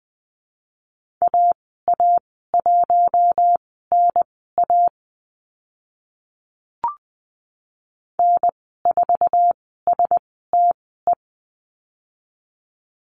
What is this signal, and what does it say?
Morse code, heard as a keyed tone, about 20 wpm, spaced as for 14 wpm, 700 Hz. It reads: AA1NA N4STE
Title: 062 Lesson 11g - Letter h - Call Signs 20wpm